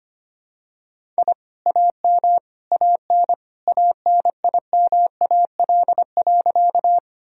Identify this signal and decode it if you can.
Morse code, heard as a keyed tone, about 25 wpm, 700 Hz. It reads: I AM AN ANIMAL.
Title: Sentences from Top 1000 Words - Review Part 1 - 25wpm